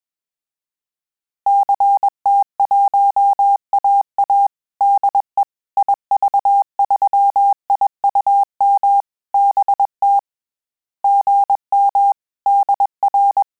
Morse code, heard as a keyed tone, about 21 wpm, 800 Hz. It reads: CT1AA DE IV3IUM BT GM DR
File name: QSO CW 100 CPM